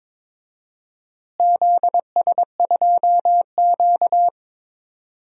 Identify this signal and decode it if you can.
Morse code, heard as a keyed tone, about 22 wpm, 700 Hz. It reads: ZS2Q
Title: DXCC-100 Call Signs - 4 Characters - Repeated 3x - 22wpm